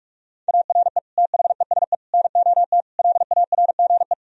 Morse code, heard as a keyed tone, about 46 wpm, 700 Hz. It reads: ARE THESE NOT LARGE
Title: Sentences from Top 1000 Words - Review Part 1 - 45wpm